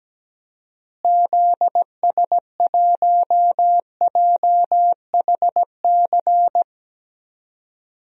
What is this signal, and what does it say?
Morse code, heard as a keyed tone, about 17 wpm, 700 Hz. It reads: ZS1JHC
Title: DXCC-100 Call Signs - 6 Characters - Repeated 3x - 17wpm